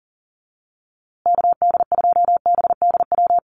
Morse code, heard as a keyed tone, about 40 wpm, 700 Hz. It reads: KD2BDW
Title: USA Call Signs Repeated 3x 40wpm